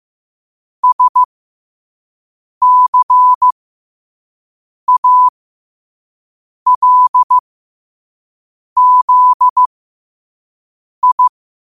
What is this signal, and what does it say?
Morse code, heard as a keyed tone, about 15 wpm, spaced as for 5 wpm, 1000 Hz. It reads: SCALZI
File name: qotdfiles / 5wpm